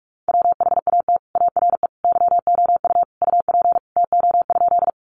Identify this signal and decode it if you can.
Morse code, heard as a keyed tone, about 44 wpm, 700 Hz. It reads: WHAT ARE YOU UP TO?
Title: Sentences from Top 100 Words 45wpm